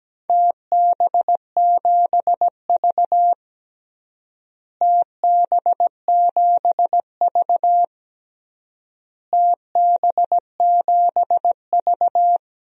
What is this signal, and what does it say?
Morse code, heard as a keyed tone, about 17 wpm, 700 Hz. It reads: TB7V TB7V TB7V
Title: DXCC-100 Call Signs - 4 Characters - Repeated 3x - 17wpm